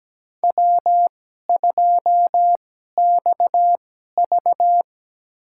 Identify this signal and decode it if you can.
Morse code, heard as a keyed tone, about 17 wpm, spaced as for 12 wpm, 700 Hz. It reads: W2XV
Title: USA Call Signs - 4 Characters - 2x Character Spacing - 17wpm